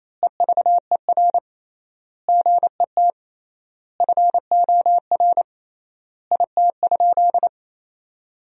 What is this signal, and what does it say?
Morse code, heard as a keyed tone, about 28 wpm, 700 Hz. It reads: EVER GET FOR IT?
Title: 3x Word-Spacing - Sentences from Top 400 Words - 28wpm